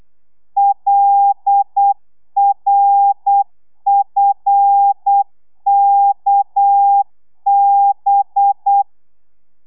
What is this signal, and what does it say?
Morse code, heard as a keyed tone, about 8 wpm, 800 Hz. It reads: LRFKB